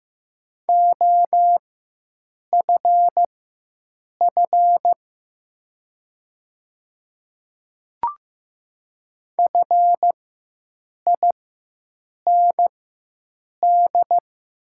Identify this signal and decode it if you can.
Morse code, heard as a keyed tone, about 15 wpm, spaced as for 7 wpm, 700 Hz. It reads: OFF FIND
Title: 4x Character Spacing - Top 200 Words - 15wpm